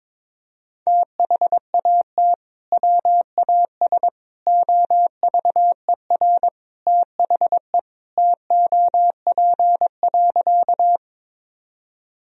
Morse code, heard as a keyed tone, about 22 wpm, 700 Hz. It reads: THAT WAS OVER THE TOP.
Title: Sentences from Top 1000 Words - Review Part 2 - 22wpm